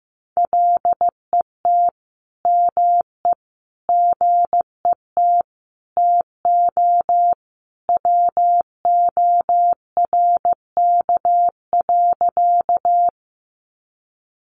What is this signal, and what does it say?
Morse code, heard as a keyed tone, about 15 wpm, 700 Hz. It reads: LET ME GET TO WORK.
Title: Sentences from Top 300 Words 15wpm